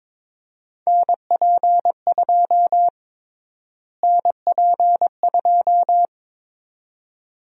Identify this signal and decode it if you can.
Morse code, heard as a keyed tone, about 22 wpm, 700 Hz. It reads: NP2 NP2
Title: USA Call Sign Prefixes - Repeated 2x - 22wpm